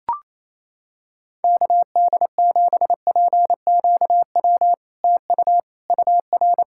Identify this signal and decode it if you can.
Morse code, heard as a keyed tone, about 28 wpm, 700 Hz. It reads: KD7PQW TU UR
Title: SOTA QSO Practice 28wpm